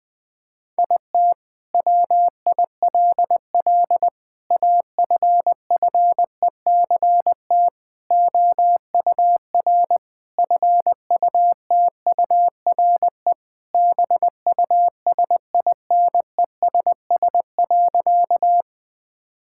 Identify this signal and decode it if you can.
Morse code, heard as a keyed tone, about 20 wpm, 700 Hz. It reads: IT WILL AFFECT OUR FUTURE BUSINESS.
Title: Sentences from Top 1000 Words - Review Part 6 - 20wpm